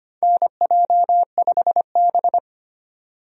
Morse code, heard as a keyed tone, about 25 wpm, 700 Hz. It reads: NJ5B
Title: USA Call Signs - 4 Characters - 25wpm